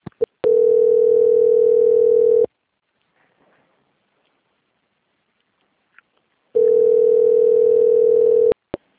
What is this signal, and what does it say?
Ringback tone (North America), 440 + 480 Hz.
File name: out- - - - - .69125